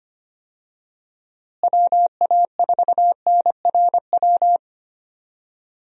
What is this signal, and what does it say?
Morse code, heard as a keyed tone, about 25 wpm, 700 Hz. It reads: WA4NRW
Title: USA Call Signs - 6 Characters - Repeated 3x - 25wpm